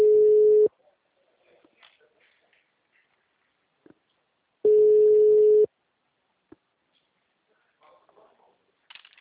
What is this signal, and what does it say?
Ringback tone (Europe, 425 Hz).